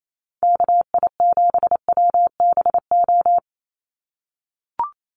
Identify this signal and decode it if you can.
Morse code, heard as a keyed tone, about 28 wpm, 700 Hz. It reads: KI7WBO
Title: USA Call Signs 28wpm